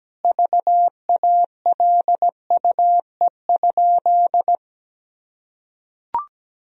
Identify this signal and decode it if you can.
Morse code, heard as a keyed tone, about 17 wpm, 700 Hz. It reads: VALUE?
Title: Sentences from Top 1000 Words - Review Part 3 - 17wpm